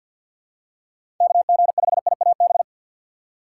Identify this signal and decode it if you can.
Morse code, heard as a keyed tone, about 51 wpm, 700 Hz. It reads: KG5IAB
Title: USA Call Signs - 4x Character Spacing - 50wpm